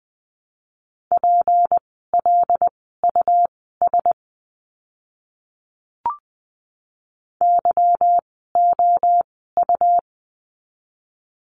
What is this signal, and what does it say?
Morse code, heard as a keyed tone, about 20 wpm, spaced as for 14 wpm, 700 Hz. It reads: PLUS YOU
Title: 153 Lesson 25f - Letter p - Words 20wpm